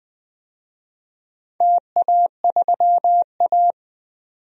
Morse code, heard as a keyed tone, about 20 wpm, 700 Hz. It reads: TA3A
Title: DXCC-100 Call Signs - 4 Characters - 20wpm